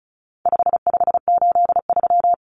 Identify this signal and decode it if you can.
Morse code, heard as a keyed tone, about 34 wpm, 700 Hz. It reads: 5583
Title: Common Cut Numbers - 4-Digit 35wpm